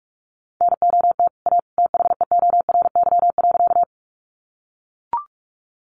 Morse code, heard as a keyed tone, about 44 wpm, 700 Hz. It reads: NOT A THEORY.
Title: Sentences from Top 600 Words 45wpm